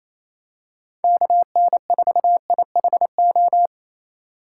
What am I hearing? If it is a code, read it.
Morse code, heard as a keyed tone, about 27 wpm, 700 Hz. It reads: KN4IHO
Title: USA Call Signs - 6 Characters - 8x Character Spacing - 28wpm